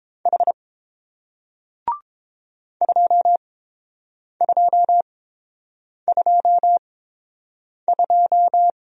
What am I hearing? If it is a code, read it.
Morse code, heard as a keyed tone, about 30 wpm, spaced as for 7 wpm, 700 Hz. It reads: H 2222